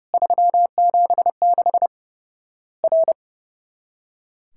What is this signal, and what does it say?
Morse code, heard as a keyed tone, about 30 wpm, 700 Hz. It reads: 376 R
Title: Worked All Europe DX Contest - QTC Exchange - 30wpm